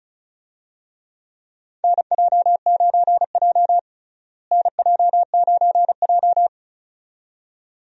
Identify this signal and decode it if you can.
Morse code, heard as a keyed tone, about 36 wpm, 700 Hz. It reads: NJ9J NJ9J
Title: USA Call Signs - 4 Characters - Repeated 2x - 35wpm